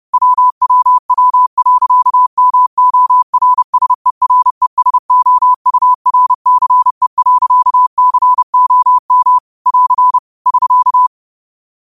Morse code, heard as a keyed tone, about 30 wpm, 1000 Hz. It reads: WWW.MORSERESOURCE.COM <AR> <SK>
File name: qotdfiles / 30wpm